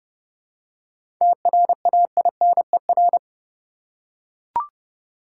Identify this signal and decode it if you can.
Morse code, heard as a keyed tone, about 30 wpm, 700 Hz. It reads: TRAINER
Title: Seven Letter Words - 30wpm